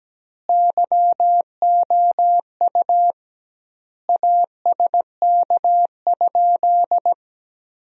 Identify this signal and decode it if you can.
Morse code, heard as a keyed tone, about 17 wpm, 700 Hz. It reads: YOU ASK?
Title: 2x Word-Spacing - Sentences from Top 200 Words - 17wpm